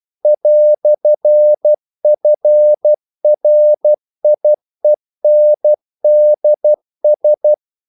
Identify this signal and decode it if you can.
Morse code, heard as a keyed tone, about 12 wpm, 600 Hz. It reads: "FRIENDS
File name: CW / Practice / EnglishQuotations0312